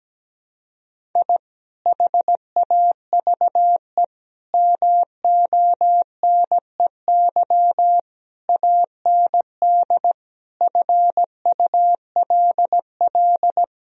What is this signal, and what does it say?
Morse code, heard as a keyed tone, about 17 wpm, 700 Hz. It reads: I HAVE MONEY AND FULL
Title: Sentences from Top 1000 Words - Review Part 3 - 17wpm